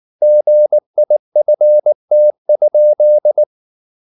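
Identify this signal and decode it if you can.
Morse code, heard as a keyed tone, about 19 wpm, 600 Hz. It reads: GIFT?
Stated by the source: Morse code at 19 wpm from CW med Gnister, reading Ken Follett's novel Never in English.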